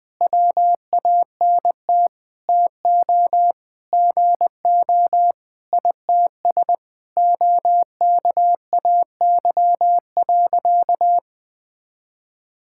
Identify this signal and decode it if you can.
Morse code, heard as a keyed tone, about 20 wpm, 700 Hz. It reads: WANT TO GO ITS OKAY.
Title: Sentences with 5-Word Phrases - Spoken Beginning - 20wpm